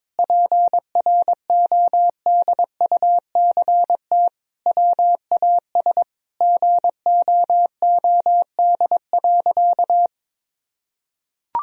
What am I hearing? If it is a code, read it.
Morse code, heard as a keyed tone, about 21 wpm, 700 Hz. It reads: PRODUCT WAS GOOD.
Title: Sentences from Top 1000 Words - Review Part 3 - 22wpm